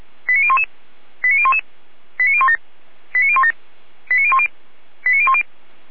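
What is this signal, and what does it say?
Five-tone selective call (ZVEI), heard as quick runs of five tones, tones of about 70 ms. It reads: 89011 89011 89017 89017 89010 89010
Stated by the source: ZVEI five-tone selective call received on SDR, from Feuerwehr Orlamünde fire brigade (f85360) audio server.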